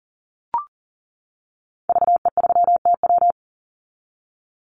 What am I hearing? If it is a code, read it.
Morse code, heard as a keyed tone, about 40 wpm, 700 Hz. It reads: VE3TW
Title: Call Signs - 40wpm